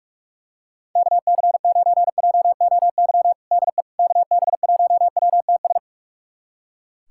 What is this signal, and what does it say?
Morse code, heard as a keyed tone, about 46 wpm, 700 Hz. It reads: KC9JOY DE KB1WTS
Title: USA Callsign DE Callsign - 6 Characters - 45wpm